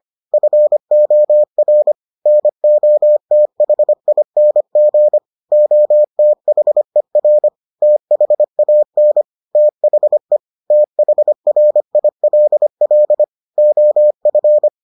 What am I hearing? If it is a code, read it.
Morse code, heard as a keyed tone, about 25 wpm, 600 Hz. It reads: FOR NOTHING OTHER THAN THE THRILL OF